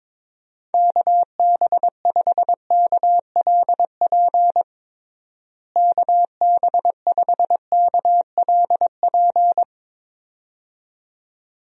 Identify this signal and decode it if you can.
Morse code, heard as a keyed tone, about 21 wpm, 700 Hz. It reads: KB5KLP KB5KLP
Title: USA Call Signs - 6 Characters - Repeated 2x - 22wpm